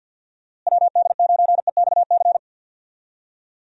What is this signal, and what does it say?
Morse code, heard as a keyed tone, about 46 wpm, 700 Hz. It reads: WD9EXC
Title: USA Call Signs - 6 Characters - Repeated 3x - 50wpm